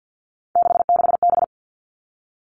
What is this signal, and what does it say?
Morse code, heard as a keyed tone, about 50 wpm, 700 Hz. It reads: 66B
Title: All Cut Numbers - 3-Digit 50wpm